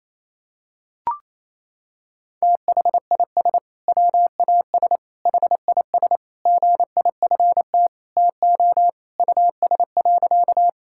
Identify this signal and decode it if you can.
Morse code, heard as a keyed tone, about 28 wpm, 700 Hz. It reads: THIS WAS HIS GIFT TO US.
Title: Sentences from Top 1000 Words - Review Part 6 - 28wpm